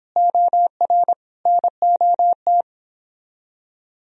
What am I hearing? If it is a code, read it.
Morse code, heard as a keyed tone, about 22 wpm, 700 Hz. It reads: OR NOT
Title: Speed-Racing - Sets of 3 Words 22wpm